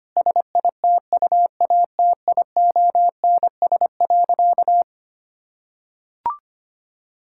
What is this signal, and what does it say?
Morse code, heard as a keyed tone, about 25 wpm, 700 Hz. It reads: SITUATIONS.
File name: practice / 25wpm / Difficult Sentences with Hints - 25wpm